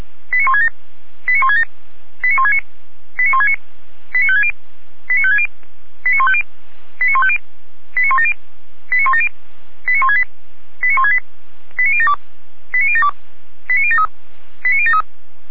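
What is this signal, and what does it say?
Five-tone selective call (ZVEI), heard as quick runs of five tones, tones of about 70 ms. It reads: 89167 89167 89169 89169 89560 89560 89150 89150 89170 89170 89168 89168 89072 89072 89073 89073
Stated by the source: ZVEI five-tone call received on an SDR, from the feuerwehr-orlamuende.org archive.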